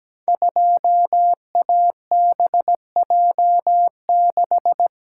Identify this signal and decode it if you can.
Morse code, heard as a keyed tone, about 17 wpm, 700 Hz. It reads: 2ABJ6
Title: Code Groups - 5 Letter:Numbers - 17wpm